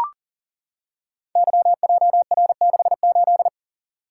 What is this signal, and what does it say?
Morse code, heard as a keyed tone, about 42 wpm, 700 Hz. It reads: YJR68
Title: Code Groups - 5 Letter:Numbers - 40wpm